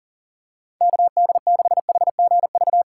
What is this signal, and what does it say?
Morse code, heard as a keyed tone, about 40 wpm, 700 Hz. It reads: KD6HGV
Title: USA Call Signs - 6 Characters - 4x Character Spacing - 40wpm